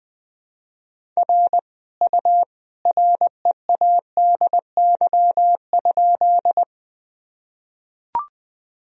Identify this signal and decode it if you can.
Morse code, heard as a keyed tone, about 20 wpm, 700 Hz. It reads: R U READY?
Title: Sentences from Top 400 Words and Abbreviations 20wpm